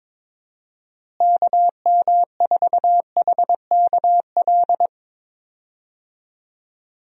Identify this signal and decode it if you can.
Morse code, heard as a keyed tone, about 22 wpm, 700 Hz. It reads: KM4HKL
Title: USA Call Signs - 6 Characters - Repeated 2x - 22wpm